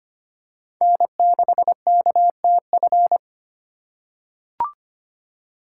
Morse code, heard as a keyed tone, about 25 wpm, 700 Hz. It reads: N6KTF
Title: USA Call Signs - 5 Characters - 4x Character Spacing - 25wpm